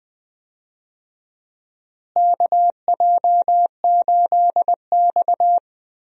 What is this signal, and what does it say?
Morse code, heard as a keyed tone, about 20 wpm, 700 Hz. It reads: KJ8X
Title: USA Call Signs - 4 Characters - Repeated 3x - 20wpm